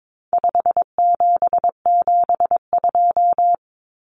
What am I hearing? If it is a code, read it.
Morse code, heard as a keyed tone, about 22 wpm, 700 Hz. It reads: 5772
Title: Common Cut Numbers - 4-Digit 22wpm